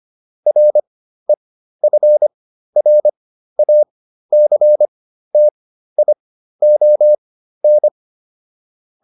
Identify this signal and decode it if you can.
Morse code, heard as a keyed tone, about 25 wpm, spaced as for 13 wpm, 600 Hz. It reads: REFRACTION
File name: SS211 13